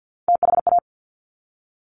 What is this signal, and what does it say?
Morse code, heard as a keyed tone, about 50 wpm, 700 Hz. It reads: THA
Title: Top Three Letter Combinations in Words 50wpm